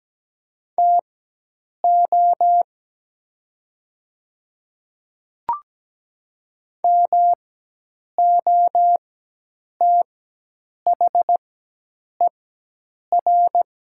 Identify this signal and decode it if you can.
Morse code, heard as a keyed tone, about 17 wpm, spaced as for 8 wpm, 700 Hz. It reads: TO MOTHER
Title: 4x Character Spacing - Top 300 Words - 17wpm